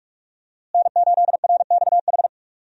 Morse code, heard as a keyed tone, about 44 wpm, 700 Hz. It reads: N8RXH
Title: USA Call Signs - 5 Characters - Repeated 3x - 45wpm